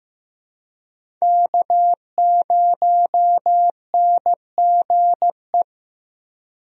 Morse code, heard as a keyed tone, about 15 wpm, 700 Hz. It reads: K0NGE
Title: USA Call Signs - 5 Characters - Repeated 2x - 15wpm